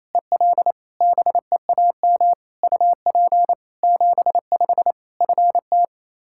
Morse code, heard as a keyed tone, about 28 wpm, 700 Hz. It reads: EL BEAM UP 75 FT
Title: K1USN SST QSO Practice 28wpm